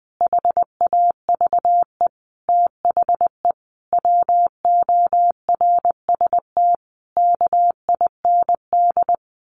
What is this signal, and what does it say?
Morse code, heard as a keyed tone, about 20 wpm, 700 Hz. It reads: HAVE THE WORST KIND
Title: A Sound Of Thunder 20wpm